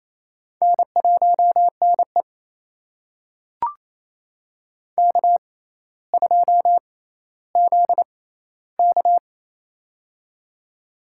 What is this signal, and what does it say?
Morse code, heard as a keyed tone, about 28 wpm, spaced as for 14 wpm, 700 Hz. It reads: N1NE K2ZK
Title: USA Call Signs - 4 Characters - 6x Character Spacing - 28wpm